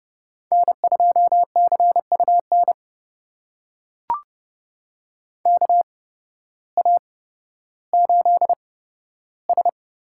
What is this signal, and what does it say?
Morse code, heard as a keyed tone, about 30 wpm, spaced as for 14 wpm, 700 Hz. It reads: N2CUN KA8S
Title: USA Call Signs - 8x Character Spacing - 30wpm